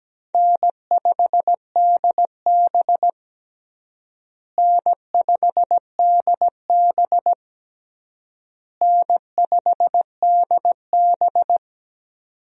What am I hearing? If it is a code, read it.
Morse code, heard as a keyed tone, about 17 wpm, 700 Hz. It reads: N5DB N5DB N5DB
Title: USA Call Signs - 4 Characters - Repeated 3x - 17wpm